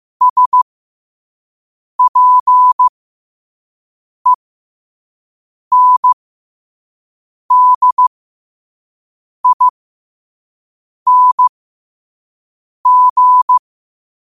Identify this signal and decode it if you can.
Morse code, heard as a keyed tone, about 15 wpm, spaced as for 5 wpm, 1000 Hz. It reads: SPENDING
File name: qotdfiles / 5wpm